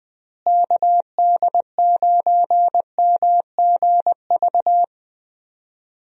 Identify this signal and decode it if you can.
Morse code, heard as a keyed tone, about 20 wpm, 700 Hz. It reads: KD9MGV
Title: USA Call Signs - 6 Characters - Repeated 2x - 20wpm